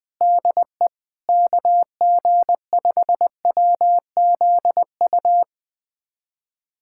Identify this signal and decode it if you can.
Morse code, heard as a keyed tone, about 20 wpm, 700 Hz. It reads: DE KG5WZU
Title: USA Callsign DE Callsign - 6 Characters - 20wpm